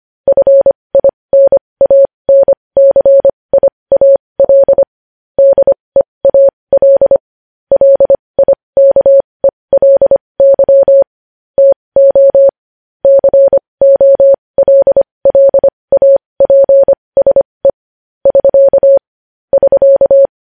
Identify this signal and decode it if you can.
Morse code, heard as a keyed tone, about 25 wpm, spaced as for 20 wpm, 550 Hz. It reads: FINANCIAL DEAL LIKELY TO COLLAPSE <SK> <SK>
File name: podcast / News headlines-1x-25x20